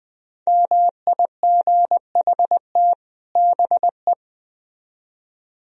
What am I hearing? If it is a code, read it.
Morse code, heard as a keyed tone, about 21 wpm, 700 Hz. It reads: MIGHT BE
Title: Speed-Racing - Sets of 2 Words 17wpm